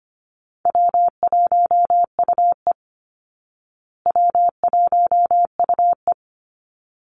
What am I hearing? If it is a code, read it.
Morse code, heard as a keyed tone, about 25 wpm, 700 Hz. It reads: W1UE W1UE
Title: Call Signs Repeated 3x - 25wpm